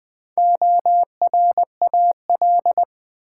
Morse code, heard as a keyed tone, about 20 wpm, 700 Hz. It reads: ORAL
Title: Build A Word - 1 Letter At A Time - 3 to 5-Letter Words - 20wpm